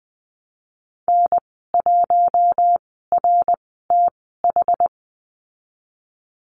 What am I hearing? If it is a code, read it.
Morse code, heard as a keyed tone, about 20 wpm, spaced as for 15 wpm, 700 Hz. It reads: N1RTH
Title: 062 Lesson 11g - Letter h - Call Signs 20wpm